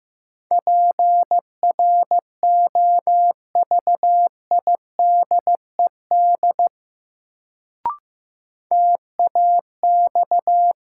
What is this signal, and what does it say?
Morse code, heard as a keyed tone, about 15 wpm, 700 Hz. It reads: PROVIDED TAX
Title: Sentences from Top 1500 Words - New Word Drill Alt - 15wpm